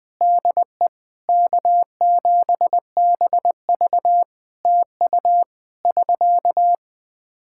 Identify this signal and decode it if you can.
Morse code, heard as a keyed tone, about 20 wpm, 700 Hz. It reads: DE K7BV TU <SK>
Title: K1USN SST QSO Practice 20wpm